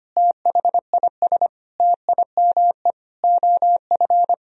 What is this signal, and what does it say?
Morse code, heard as a keyed tone, about 25 wpm, 700 Hz. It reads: THIS TIME OF
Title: Sentences from Top 1000 Words - Review Part 3 - 25wpm